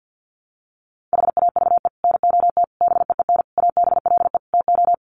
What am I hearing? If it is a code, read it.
Morse code, heard as a keyed tone, about 50 wpm, 700 Hz. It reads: HAVE NOT BEEN ABLE TO
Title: Sets of 5 Words Encore 50wpm